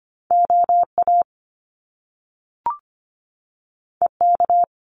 Morse code, heard as a keyed tone, about 25 wpm, 700 Hz. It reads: O A E K
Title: Top Two Letter Combinations in Words 25wpm